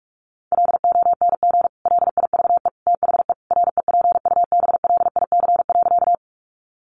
Morse code, heard as a keyed tone, about 44 wpm, 700 Hz. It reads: LONG LIVE THE REPUBLIC.
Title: Sentences from Top 1500 Words 45wpm